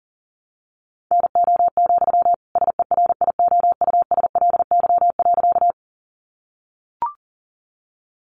Morse code, heard as a keyed tone, about 40 wpm, 700 Hz. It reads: NO, SERIOUSLY.
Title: Sentences from Top 1200 Words 40wpm